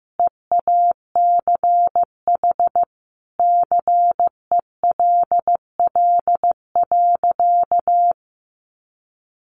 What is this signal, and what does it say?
Morse code, heard as a keyed tone, about 15 wpm, 700 Hz. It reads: EACH CELL.
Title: Sentences from Top 1100 Words 15wpm